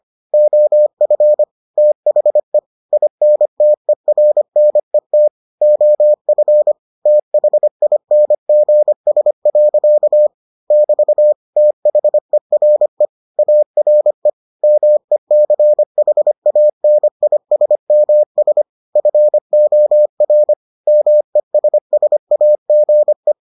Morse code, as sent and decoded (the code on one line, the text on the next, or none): --- ..-. / - .... / .. -. - . .-. -. - / --- ..-. / - .... .. -. --. ... .-.-.- / -...- / - .... .-. / .- .-. / -- . -.-. .... .- -. .. ... -- ... / ..-. --- .-. / -- . ... ... .- --. .
OF THE INTERNET OF THINGS. <BT> THERE ARE MECHANISMS FOR MESSAGE